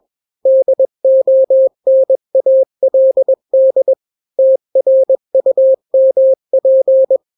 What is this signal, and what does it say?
Morse code, heard as a keyed tone, about 21 wpm, 530 Hz. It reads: DONALD TRUMP
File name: daily / 20wpm